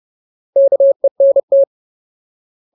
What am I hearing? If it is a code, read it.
Morse code, heard as a keyed tone, about 30 wpm, 550 Hz. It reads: KENT
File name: Ultimate Word List / 30WPM Ultimate